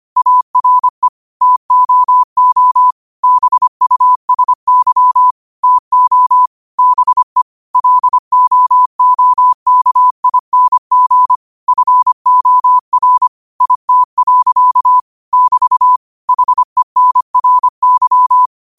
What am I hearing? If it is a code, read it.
Morse code, heard as a keyed tone, about 25 wpm, 1000 Hz. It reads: ARE TOO BUSY TO BE LOOKING FOR IT. <BT> HENRY